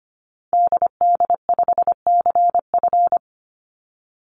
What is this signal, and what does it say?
Morse code, heard as a keyed tone, about 25 wpm, 700 Hz. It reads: DD5CF
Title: Call Signs - 25wpm